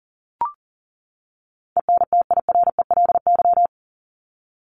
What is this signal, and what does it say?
Morse code, heard as a keyed tone, about 37 wpm, 700 Hz. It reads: ENTIRELY
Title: LY Suffix - Top 100 Words with Base Words 40wpm